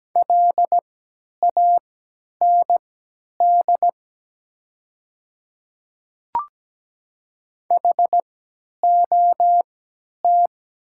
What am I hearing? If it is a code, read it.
Morse code, heard as a keyed tone, about 17 wpm, spaced as for 10 wpm, 700 Hz. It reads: LAND HOT
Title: 3x Character Spacing - Top 200 Words - 17wpm